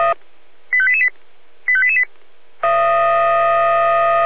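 Five-tone selective call (ZVEI), heard as quick runs of five tones, tones of about 70 ms. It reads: 85888 85888
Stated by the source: ZVEI five-tone selective call received on an SDR.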